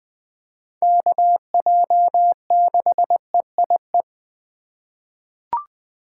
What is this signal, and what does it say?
Morse code, heard as a keyed tone, about 20 wpm, 700 Hz. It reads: KJ6EIE
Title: USA Call Signs - 6 Characters - 8x Character Spacing - 20wpm